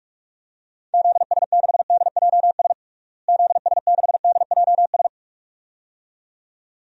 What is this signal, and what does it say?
Morse code, heard as a keyed tone, about 41 wpm, 700 Hz. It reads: ZS6DJS ZS6DJS
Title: DXCC-100 Call Signs - 6 Characters - Repeated 2x - 45wpm